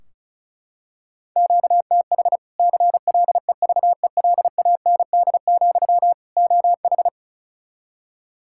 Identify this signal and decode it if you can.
Morse code, heard as a keyed tone, about 34 wpm, 700 Hz. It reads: QTH CLEVELAND, OH
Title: Practice QSO - Exchange 1 - Beginning - 35wpm